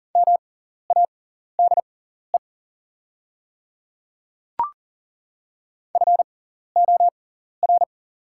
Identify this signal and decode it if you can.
Morse code, heard as a keyed tone, about 40 wpm, spaced as for 14 wpm, 700 Hz. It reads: MADE FOR
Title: 6x Character Spacing - Top 200 Words - 40wpm